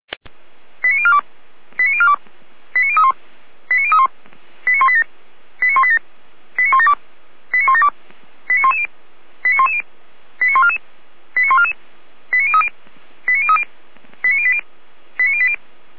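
Five-tone selective call (ZVEI), heard as quick runs of five tones, tones of about 70 ms. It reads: 89042 89042 89031 89031 89187 89187 89172 89172 89110 89110 89144 89144 89030 89030 89080 89080
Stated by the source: ZVEI five-tone selective call received on SDR.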